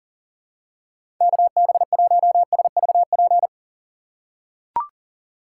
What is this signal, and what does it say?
Morse code, heard as a keyed tone, about 40 wpm, 700 Hz. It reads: KB1SVP
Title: USA Call Signs - 6 Characters - 2x Character Spacing - 40wpm